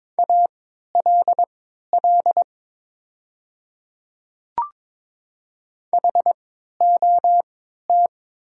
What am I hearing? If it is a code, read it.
Morse code, heard as a keyed tone, about 22 wpm, spaced as for 12 wpm, 700 Hz. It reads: ALL HOT
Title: 3x Character Spacing - Top 200 Words - 22wpm